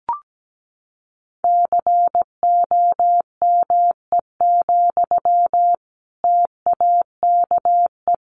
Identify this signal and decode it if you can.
Morse code, heard as a keyed tone, about 17 wpm, 700 Hz. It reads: COME, TAKE
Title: Sentences from Top 1200 Words 17wpm